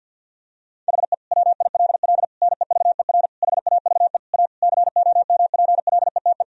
Morse code, heard as a keyed tone, about 52 wpm, 700 Hz. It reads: HE WILL NEVER HAVE A COMPLETE
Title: Sentences from Top 1000 Words - Review Part 2 - 50wpm